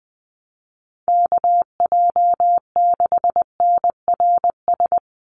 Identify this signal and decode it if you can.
Morse code, heard as a keyed tone, about 19 wpm, 700 Hz. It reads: KJ6NRS
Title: USA Call Signs 20wpm